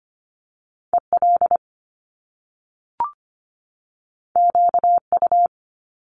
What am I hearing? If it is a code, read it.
Morse code, heard as a keyed tone, about 25 wpm, 700 Hz. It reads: EL QU
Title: Top Two Letter Combinations in Words 25wpm